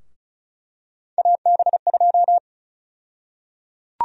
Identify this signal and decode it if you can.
Morse code, heard as a keyed tone, about 36 wpm, 700 Hz. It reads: AB2
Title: USA Call Sign Prefixes - 35wpm